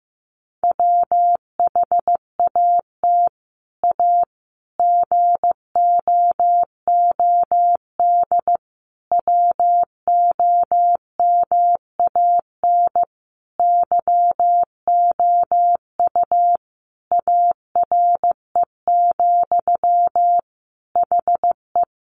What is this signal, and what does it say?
Morse code, heard as a keyed tone, about 15 wpm, 700 Hz. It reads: WHAT A GOOD WOMAN YOU ARE, HE
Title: Sentences from Top 1100 Words 15wpm